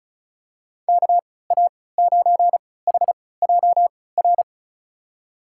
Morse code, heard as a keyed tone, about 36 wpm, spaced as for 20 wpm, 700 Hz. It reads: KA9HJR
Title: USA Call Signs - 6 Characters - 3x Character Spacing - 35wpm